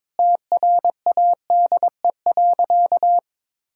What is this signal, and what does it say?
Morse code, heard as a keyed tone, about 22 wpm, 700 Hz. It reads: TRADE.
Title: Sentences from Top 1000 Words - Review Part 4 - 22wpm